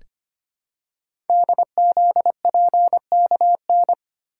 Morse code, heard as a keyed tone, about 25 wpm, 700 Hz. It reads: DZPKN
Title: Code Groups - 5 Letters - 25wpm